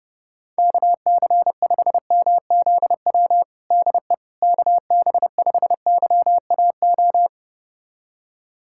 Morse code, heard as a keyed tone, about 32 wpm, 700 Hz. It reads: KC5MZW DE KB5YAO
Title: USA Callsign DE Callsign - 6 Characters - 30wpm